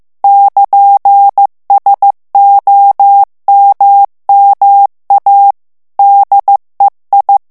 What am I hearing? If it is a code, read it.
Morse code, heard as a keyed tone, about 15 wpm, 800 Hz. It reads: <KN>SOMMA DEI
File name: morse mp3 / ITALIEN texte clair 2 vitesse 14 mots minute